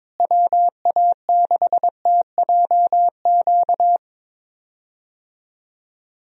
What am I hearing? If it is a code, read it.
Morse code, heard as a keyed tone, about 22 wpm, 700 Hz. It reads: WA6TJQ
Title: USA Call Signs - 6 Characters - Repeated 3x - 22wpm